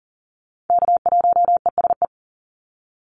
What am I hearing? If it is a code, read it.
Morse code, heard as a keyed tone, about 40 wpm, 700 Hz. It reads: K1ESE
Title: Call Signs - 40wpm